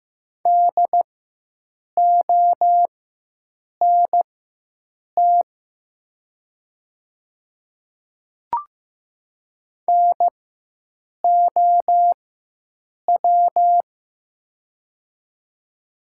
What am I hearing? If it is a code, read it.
Morse code, heard as a keyed tone, about 15 wpm, spaced as for 7 wpm, 700 Hz. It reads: DONT NOW
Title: 4x Character Spacing - Top 300 Words - 15wpm